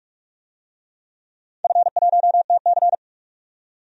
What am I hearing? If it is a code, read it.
Morse code, heard as a keyed tone, about 42 wpm, 700 Hz. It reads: F1TC
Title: DXCC-100 Call Signs - 4 Characters - 45wpm